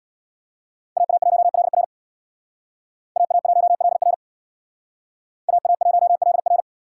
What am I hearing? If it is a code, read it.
Morse code, heard as a keyed tone, about 64 wpm, 700 Hz. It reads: AA1LR AA1LR AA1LR